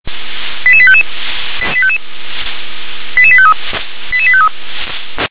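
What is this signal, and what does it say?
Five-tone selective call (ZVEI), heard as quick runs of five tones, tones of about 70 ms. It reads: 88855 88853 88853
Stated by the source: SDR-received ZVEI five-tone selective call.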